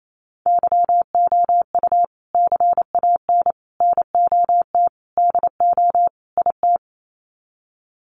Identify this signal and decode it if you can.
Morse code, heard as a keyed tone, about 28 wpm, 700 Hz. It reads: YOU CAN NOT DO IT
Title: Sets of 5 Words Encore2 28wpm